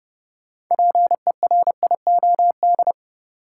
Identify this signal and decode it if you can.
Morse code, heard as a keyed tone, about 32 wpm, 700 Hz. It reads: PERIOD
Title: Sets of 5 Words - New Word Drill - 30wpm